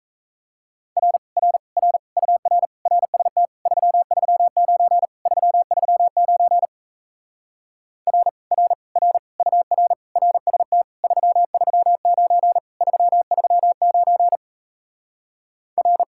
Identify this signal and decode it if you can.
Morse code, heard as a keyed tone, about 38 wpm, 700 Hz. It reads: R R R UR RST 339 339 R R R UR RST 339 339 R